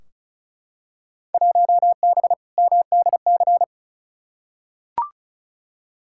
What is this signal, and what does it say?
Morse code, heard as a keyed tone, about 34 wpm, 700 Hz. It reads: 1B MDC
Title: Field Day Exchanges - 35wpm